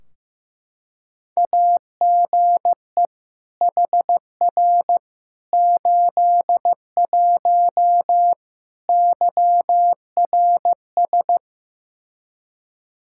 Morse code, heard as a keyed tone, about 15 wpm, 700 Hz. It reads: AGE HR 81 YRS
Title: Practice QSO - Exchange 3 - More Info - 15wpm